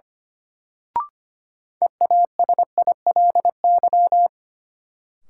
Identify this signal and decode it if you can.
Morse code, heard as a keyed tone, about 25 wpm, 700 Hz. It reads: EASILY
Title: Sentences from Top 900 Words - New Word Drill - 25wpm